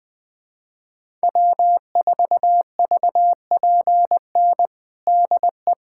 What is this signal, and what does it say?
Morse code, heard as a keyed tone, about 21 wpm, 700 Hz. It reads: W4VPN DE
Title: USA Callsign DE Callsign - 5 Characters - 20wpm